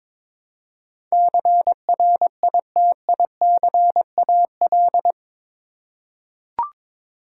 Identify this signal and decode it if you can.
Morse code, heard as a keyed tone, about 21 wpm, 700 Hz. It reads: CRITICAL
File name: practice / 22wpm / Eight Letter Words - 22wpm